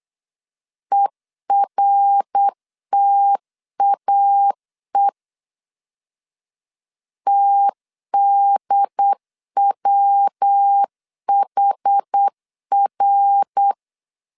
Morse code, heard as a keyed tone, about 8 wpm, 800 Hz. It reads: ERTAE TDWHR